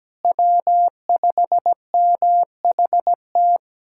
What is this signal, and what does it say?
Morse code, heard as a keyed tone, about 17 wpm, 700 Hz. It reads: W5MHT
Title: USA Callsign DE Callsign - 5 Characters - 17wpm